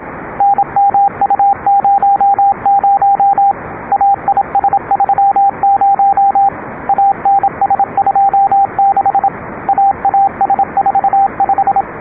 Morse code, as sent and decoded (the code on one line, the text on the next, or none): -. -- ..- ----- ----- / .- .. ... ...-- ----- / .- -. ... ..--- -.... / .- .- ... ....- .....
NMU00 AIS30 ANS26 AAS45